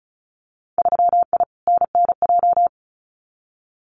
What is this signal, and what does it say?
Morse code, heard as a keyed tone, about 36 wpm, 700 Hz. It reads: 3I NNJ
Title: Winter Field Day Exchanges 35wpm